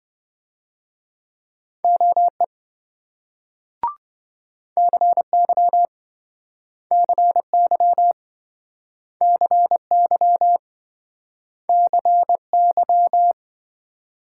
Morse code, heard as a keyed tone, about 26 wpm, 700 Hz. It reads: OE CY CY CY CY